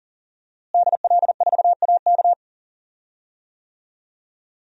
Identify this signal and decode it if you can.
Morse code, heard as a keyed tone, about 40 wpm, 690 Hz. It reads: DL4AK
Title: DXCC-100 Call Signs - 5 Characters - Repeated 2x - 40wpm